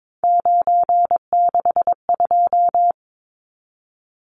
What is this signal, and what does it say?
Morse code, heard as a keyed tone, about 22 wpm, 700 Hz. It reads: 962
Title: Numbers - 3-Digit 22wpm